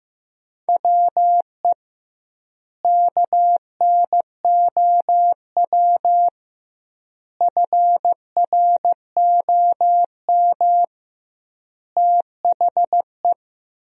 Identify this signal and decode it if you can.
Morse code, heard as a keyed tone, about 15 wpm, 700 Hz. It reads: WE KNOW FROM THE
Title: 2x Word-Spacing - Sentences from Top 500 Words - 15wpm